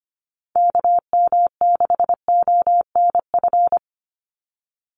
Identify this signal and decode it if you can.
Morse code, heard as a keyed tone, about 25 wpm, 700 Hz. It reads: KM6ONF
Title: USA Call Signs 25wpm